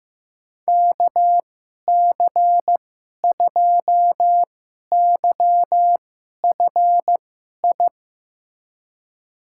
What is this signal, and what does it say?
Morse code, heard as a keyed tone, about 16 wpm, spaced as for 11 wpm, 700 Hz. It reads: KC2YFI
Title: USA Call Signs - 6 Characters - 2x Character Spacing - 15wpm